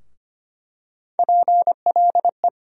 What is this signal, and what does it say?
Morse code, heard as a keyed tone, about 25 wpm, 700 Hz. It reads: PLE
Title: USA Call Sign Postfixes - 3 Letters 25wpm